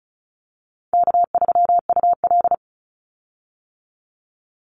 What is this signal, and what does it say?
Morse code, heard as a keyed tone, about 36 wpm, 700 Hz. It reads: K3UL
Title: Call Signs - 35wpm